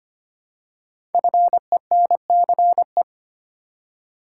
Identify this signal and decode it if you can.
Morse code, heard as a keyed tone, about 25 wpm, 700 Hz. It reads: FENCE
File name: 5 Letter Words - First 3 Letters Given - 25wpm